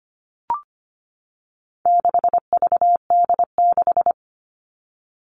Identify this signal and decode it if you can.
Morse code, heard as a keyed tone, about 25 wpm, 700 Hz. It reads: 6VD6
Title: All Cut Numbers - 4-Digit 25wpm